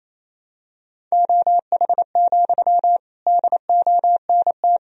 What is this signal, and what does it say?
Morse code, heard as a keyed tone, about 28 wpm, 700 Hz. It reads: OH, DONT
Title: Sentences from Top 1000 Words - Review Part 4 - 28wpm